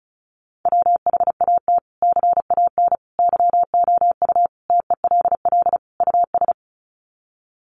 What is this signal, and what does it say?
Morse code, heard as a keyed tone, about 34 wpm, 700 Hz. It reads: WHAT CAN YOU TELL US
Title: Sets of 5 Words Encore 35wpm